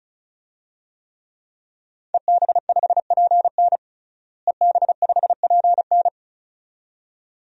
Morse code, heard as a keyed tone, about 36 wpm, 700 Hz. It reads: EB5PN EB5PN
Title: DXCC-100 Call Signs - 5 Characters - Repeated 2x - 35wpm